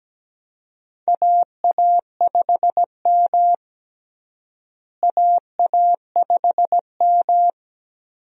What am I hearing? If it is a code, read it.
Morse code, heard as a keyed tone, about 17 wpm, 700 Hz. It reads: AA5M AA5M
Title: USA Call Signs - 4 Characters - Repeated 2x - 17wpm